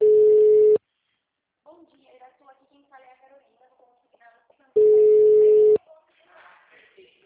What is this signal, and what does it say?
Ringback tone (Europe, 425 Hz), with speech.
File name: out- -2027- - - .1179